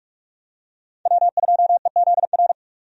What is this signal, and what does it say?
Morse code, heard as a keyed tone, about 45 wpm, 700 Hz. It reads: W2EZR
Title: USA Call Signs - 5 Characters - 45wpm